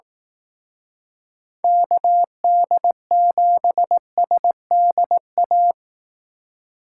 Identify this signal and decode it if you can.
Morse code, heard as a keyed tone, about 18 wpm, 700 Hz. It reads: KD7SDA
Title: Speed Racing - USA 2x3 Call Signs 15wpm